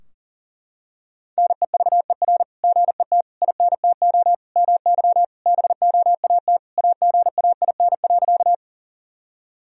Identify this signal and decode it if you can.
Morse code, heard as a keyed tone, about 40 wpm, 700 Hz. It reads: NEVER GET INTO MY BOAT AGAIN.
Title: Sentences from Top 1000 Words - Review Part 3 - 40wpm